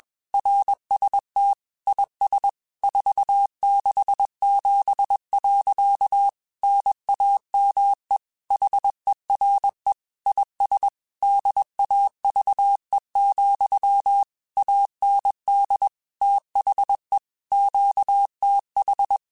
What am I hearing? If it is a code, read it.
Morse code, heard as a keyed tone, about 21 wpm, 800 Hz. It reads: RST IS 467. NAME HERE IS DAVE, AND THE QTH
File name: QSOs / QSOs-mp3s / WA7QOX 2@20